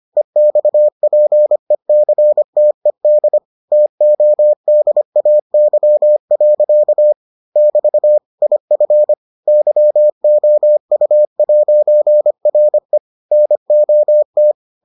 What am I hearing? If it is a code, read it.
Morse code, heard as a keyed tone, about 25 wpm, 600 Hz. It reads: EXPECTED TODAY. <BT> IF YOU'RE NOT